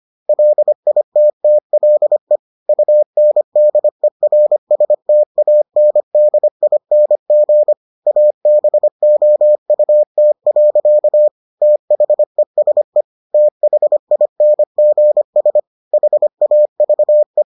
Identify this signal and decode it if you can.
Morse code, heard as a keyed tone, about 25 wpm, 600 Hz. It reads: LITTLE UNDERSTANDING ABOUT. THESE THINGS HAVE